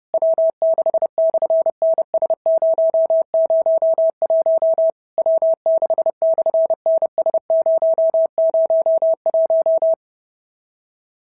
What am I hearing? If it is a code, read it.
Morse code, heard as a keyed tone, about 30 wpm, 650 Hz. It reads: W6/NS001 W6/NS001
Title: SOTA QSO Practice 30wpm